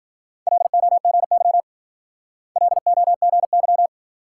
Morse code, heard as a keyed tone, about 49 wpm, 700 Hz. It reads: LOGY LOGY
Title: Speed-Racing Multi-letter Phonemes - 40wpm